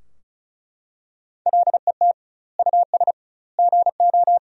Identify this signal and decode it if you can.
Morse code, heard as a keyed tone, about 34 wpm, 700 Hz. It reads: LET US GO
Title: 2x Word-Spacing - Sentences from Top 300 Words - 35wpm